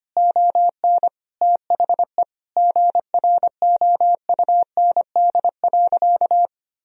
Morse code, heard as a keyed tone, about 25 wpm, 700 Hz. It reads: ON THE GROUND.
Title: Sentences from Top 1000 Words - Review Part 2 - 25wpm